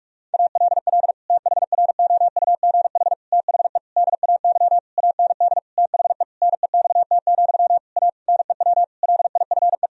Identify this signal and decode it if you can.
Morse code, heard as a keyed tone, about 41 wpm, 700 Hz. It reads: ALL THROUGH THE DAY AND THE NEXT, A NEW LIFE
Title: Sentences from Top 1000 Words - Review Part 1 - 45wpm